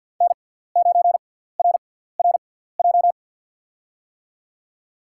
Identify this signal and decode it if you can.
Morse code, heard as a keyed tone, about 53 wpm, spaced as for 17 wpm, 700 Hz. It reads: N9RRJ
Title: USA Call Signs - 6x Character Spacing - 50wpm